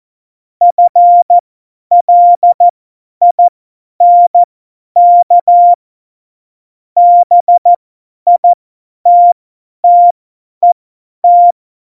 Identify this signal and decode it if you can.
Morse code, heard as a keyed tone, about 14 wpm, spaced as for 10 wpm, 700 Hz. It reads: FLINK BITTET
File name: Morsekurs / Witze / Witz 08. September 2020, 50 F70